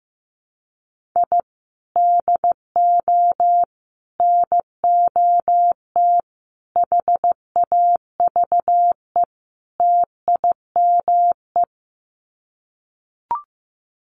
Morse code, heard as a keyed tone, about 15 wpm, 700 Hz. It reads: I DO NOT HAVE TIME
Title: Sets of 5 Words Encore 15wpm